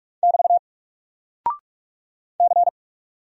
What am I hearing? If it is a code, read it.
Morse code, heard as a keyed tone, about 44 wpm, 700 Hz. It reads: <BT> C
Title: Single Character (Letter:Number:Punctuation:Prosign) Alt 45wpm